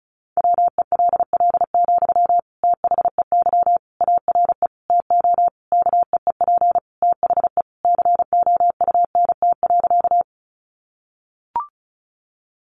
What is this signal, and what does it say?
Morse code, heard as a keyed tone, about 34 wpm, 700 Hz. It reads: WELL, THEY ARE TO KEEP THE COUNT.
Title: Sentences from Top 1500 Words 35wpm